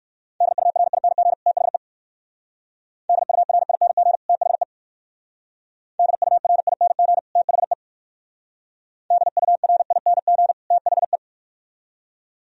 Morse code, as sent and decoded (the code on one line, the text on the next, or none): -.. ..- .-. .. -. --. / - .... / -.. ..- .-. .. -. --. / - .... / -.. ..- .-. .. -. --. / - .... / -.. ..- .-. .. -. --. / - .... .
DURING THE DURING THE DURING THE DURING THE